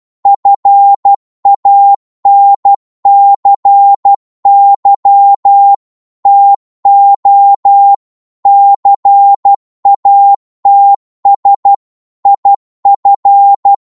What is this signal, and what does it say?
Morse code, heard as a keyed tone, about 12 wpm, 800 Hz. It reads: FANCY TO CATS IF